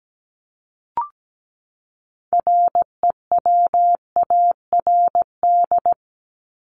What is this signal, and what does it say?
Morse code, heard as a keyed tone, about 17 wpm, 700 Hz. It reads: REWARD
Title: Top 1751-2000 Words 17wpm